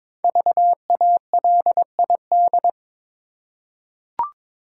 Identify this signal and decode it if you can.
Morse code, heard as a keyed tone, about 22 wpm, 700 Hz. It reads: VALID
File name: 5 Letter Words - First 3 Letters Given - 22wpm